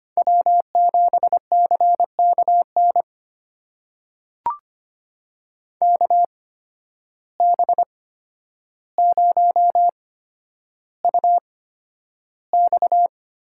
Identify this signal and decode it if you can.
Morse code, heard as a keyed tone, about 25 wpm, spaced as for 11 wpm, 700 Hz. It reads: W7CKN KB0UX
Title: USA Call Signs - 5 Characters - 8x Character Spacing - 25wpm